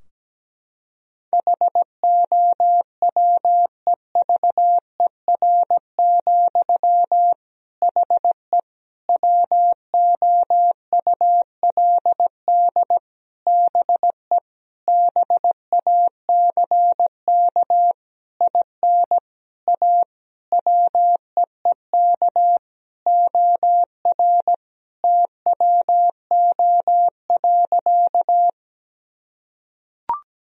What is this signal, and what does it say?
Morse code, heard as a keyed tone, about 17 wpm, 700 Hz. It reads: HOWEVER, HE WOULD BE BACK IN A WEEK OR TWO.
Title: Sentences from Top 1000 Words - Review Part 2 - 17wpm